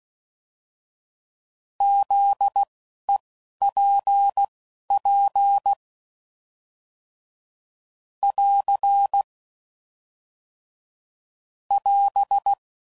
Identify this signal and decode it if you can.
Morse code, heard as a keyed tone, about 16 wpm, spaced as for 11 wpm, 800 Hz. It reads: ZEPP <AR> <AS>
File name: CW Code Groups @10WPM / fromSM's / Antennas